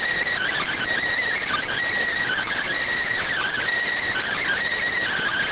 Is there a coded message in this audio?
Yes: Emergency Alert System (SAME) tones (header bursts) from 0.3 to 4.6 s, heard as harsh buzzing bursts.